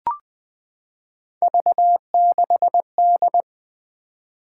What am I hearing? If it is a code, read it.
Morse code, heard as a keyed tone, about 21 wpm, 700 Hz. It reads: V6D
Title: Three Letter-Number Combinations - 20wpm